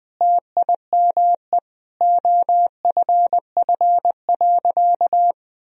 Morse code, heard as a keyed tone, about 21 wpm, 700 Hz. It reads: TIME OFF.
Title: Sentences from Top 1000 Words - Review Part 3 - 20wpm